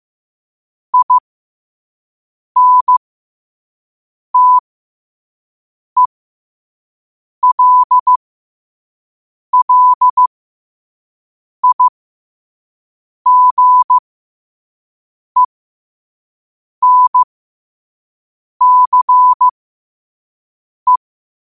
Morse code, heard as a keyed tone, about 15 wpm, spaced as for 5 wpm, 1000 Hz. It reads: INTELLIGENCE